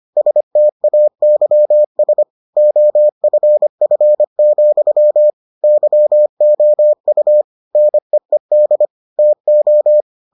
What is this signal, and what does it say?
Morse code, heard as a keyed tone, about 25 wpm, 600 Hz. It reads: STAYS OFF, YOU NEED TO